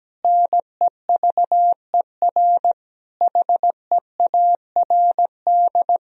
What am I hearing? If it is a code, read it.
Morse code, heard as a keyed tone, about 17 wpm, 700 Hz. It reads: NEVER HEARD
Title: Sentences from Top 1000 Words - Review Part 1 - 17wpm